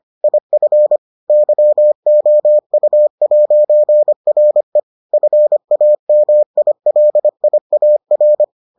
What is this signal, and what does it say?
Morse code, heard as a keyed tone, about 25 wpm, 600 Hz. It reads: IF YOU'RE FAMILIAR